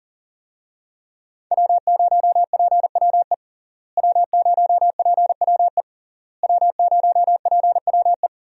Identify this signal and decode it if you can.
Morse code, heard as a keyed tone, about 40 wpm, 700 Hz. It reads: W0PWE W0PWE W0PWE
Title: USA Call Signs - 5 Characters - Repeated 3x - 40wpm